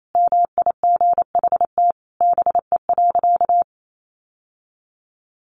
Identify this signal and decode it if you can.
Morse code, heard as a keyed tone, about 28 wpm, 700 Hz. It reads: MIGHT BE.
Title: Sentences from Top 1500 Words 28wpm